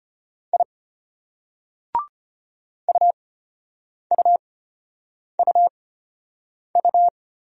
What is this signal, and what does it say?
Morse code, heard as a keyed tone, about 34 wpm, spaced as for 8 wpm, 700 Hz. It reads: I UUUU